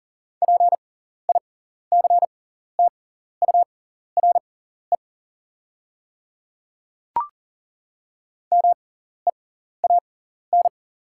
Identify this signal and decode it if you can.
Morse code, heard as a keyed tone, about 40 wpm, spaced as for 14 wpm, 700 Hz. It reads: PICTURE MEAN
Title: 6x Character Spacing - Top 200 Words - 40wpm